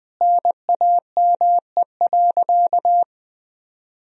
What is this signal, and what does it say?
Morse code, heard as a keyed tone, about 20 wpm, 700 Hz. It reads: NAME.
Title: Sentences from Top 200 Words and Abbreviations 20wpm